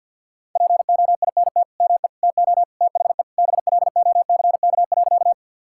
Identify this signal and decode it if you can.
Morse code, heard as a keyed tone, about 50 wpm, 700 Hz. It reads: POINT ME TO THE BLOCK.
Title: Sentences from Top 1600 Words 50wpm